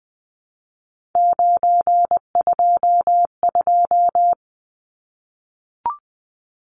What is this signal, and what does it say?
Morse code, heard as a keyed tone, about 20 wpm, 700 Hz. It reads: 922
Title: Numbers - 3-Digit 20wpm